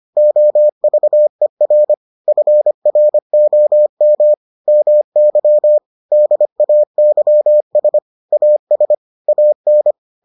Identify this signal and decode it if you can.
Morse code, heard as a keyed tone, about 25 wpm, 600 Hz. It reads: OVER FROM MY DAYS AS AN